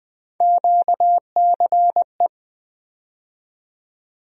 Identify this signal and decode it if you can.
Morse code, heard as a keyed tone, about 20 wpm, 700 Hz. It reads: QCE
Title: USA Call Sign Postfixes - 3 Letters 20wpm